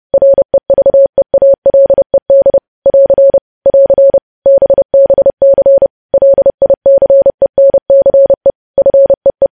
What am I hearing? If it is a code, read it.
Morse code, heard as a keyed tone, about 30 wpm, 550 Hz. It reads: REVEALED <AR> <AR> BBC LICENCE FEE
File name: podcast / News headlines-1x-30x30